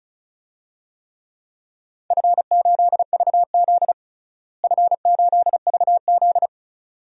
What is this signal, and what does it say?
Morse code, heard as a keyed tone, about 36 wpm, 700 Hz. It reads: F8VZ F8VZ
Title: DXCC-100 Call Signs - 4 Characters - Repeated 2x - 35wpm